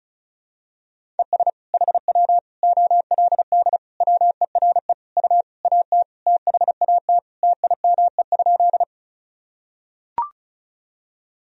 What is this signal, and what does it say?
Morse code, heard as a keyed tone, about 36 wpm, 700 Hz. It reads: ES HW OLD WERE U AT THAT TIME?
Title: Sentences from Top 200 Words and Abbreviations 35wpm